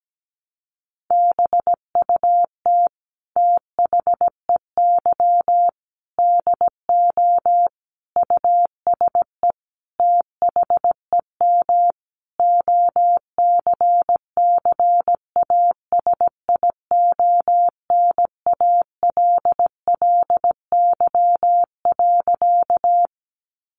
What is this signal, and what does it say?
Morse code, heard as a keyed tone, about 17 wpm, 700 Hz. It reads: BUT THEY DO USE THEM OCCASIONALLY.
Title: Sentences from Top 1300 Words 17wpm